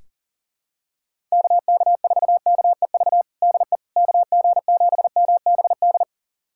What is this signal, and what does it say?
Morse code, heard as a keyed tone, about 40 wpm, 700 Hz. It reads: KK4KEV DE KG7MBD
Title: USA Callsign DE Callsign - 6 Characters - 40wpm